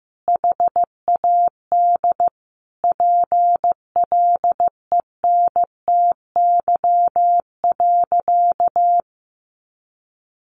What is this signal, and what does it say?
Morse code, heard as a keyed tone, about 15 wpm, 700 Hz. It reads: HAD PLENTY.
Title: Sentences from Top 1300 Words 15wpm